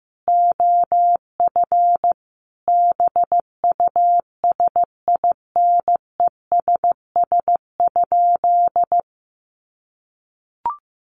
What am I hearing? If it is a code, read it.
Morse code, heard as a keyed tone, about 15 wpm, 700 Hz. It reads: OF BUSINESS?
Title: Sentences from Top 1300 Words 15wpm